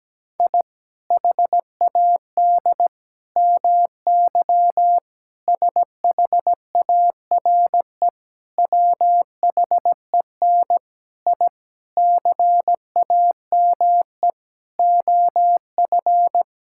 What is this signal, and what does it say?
Morse code, heard as a keyed tone, about 17 wpm, 700 Hz. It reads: I HAD MY SHARE WHEN I CAME OF
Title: Sentences from Top 1000 Words - Review Part 6 - 17wpm